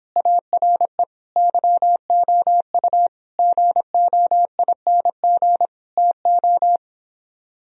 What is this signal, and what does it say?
Morse code, heard as a keyed tone, about 29 wpm, 700 Hz. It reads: ARE YOU GOING TO